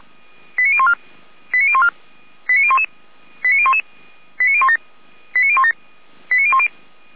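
Five-tone selective call (ZVEI), heard as quick runs of five tones, tones of about 70 ms. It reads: 89014 89014 89011 89011 89017 89017 89010
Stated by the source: ZVEI five-tone call received by SDR.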